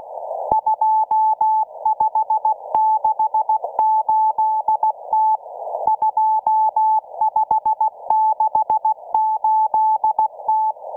Morse code, heard as a keyed tone, about 16 wpm, 870 Hz. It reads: 2568T 2568T